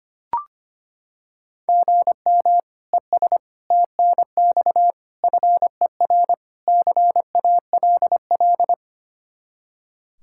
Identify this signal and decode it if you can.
Morse code, heard as a keyed tone, about 25 wpm, 700 Hz. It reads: GM ES TNX FER CALL
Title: Practice QSO - Exchange 1 - Beginning - 25wpm